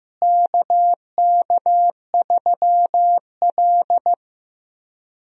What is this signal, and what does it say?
Morse code, heard as a keyed tone, about 15 wpm, 700 Hz. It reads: KK3L
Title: USA Call Signs - 4 Characters - Repeated 2x - 15wpm